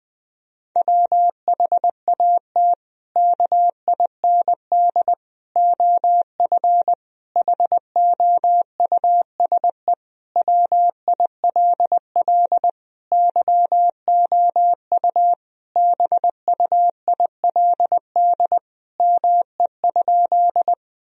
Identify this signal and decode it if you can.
Morse code, heard as a keyed tone, about 20 wpm, 700 Hz. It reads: WHAT KIND OF HOUSE WILL YOU BUILD ME?
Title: Sentences from Top 1000 Words - Review Part 1 - 20wpm